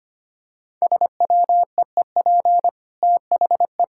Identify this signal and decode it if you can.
Morse code, heard as a keyed tone, about 25 wpm, 700 Hz. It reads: SWEEP THE
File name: Easy Sentences with Hints - 25wpm